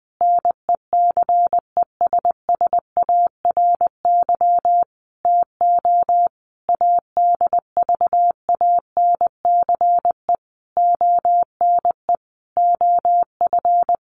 Morse code, heard as a keyed tone, about 20 wpm, 700 Hz. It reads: NECESSARY TO ADVANCE ONE OF
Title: Sentences from Top 1400 Words 20wpm